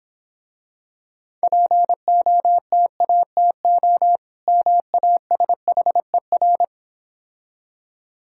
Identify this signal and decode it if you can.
Morse code, heard as a keyed tone, about 26 wpm, 700 Hz. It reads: POTATO MASHER